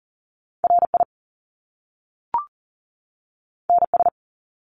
Morse code, heard as a keyed tone, about 41 wpm, 700 Hz. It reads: RI NS
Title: Top Two Letter Combinations in Words 40wpm